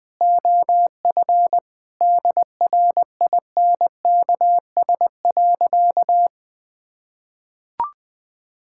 Morse code, heard as a keyed tone, about 21 wpm, 700 Hz. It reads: OF DRINKS.
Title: Sentences with 6-Word Phrases - 20wpm